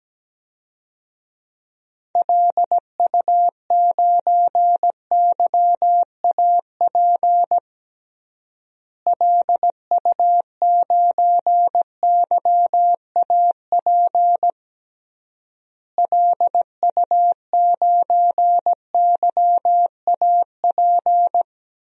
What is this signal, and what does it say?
Morse code, heard as a keyed tone, about 17 wpm, 700 Hz. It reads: LU9YAP LU9YAP LU9YAP
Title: DXCC-100 Call Signs - 6 Characters - Repeated 3x - 17wpm